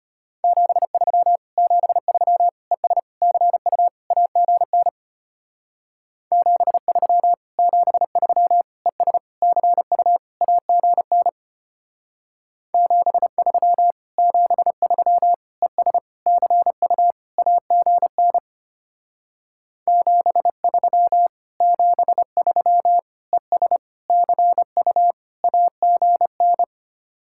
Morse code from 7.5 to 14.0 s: 73 ES CU AGN 73

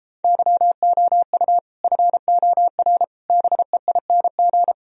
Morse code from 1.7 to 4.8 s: FOR BEING

